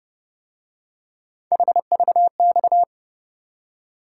Morse code, heard as a keyed tone, about 29 wpm, 700 Hz. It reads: HVX